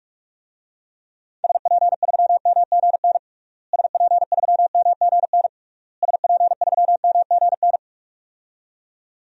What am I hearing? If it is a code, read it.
Morse code, heard as a keyed tone, about 46 wpm, 700 Hz. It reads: SP3MGN SP3MGN SP3MGN
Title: DXCC-100 Call Signs - 6 Characters - Repeated 3x - 45wpm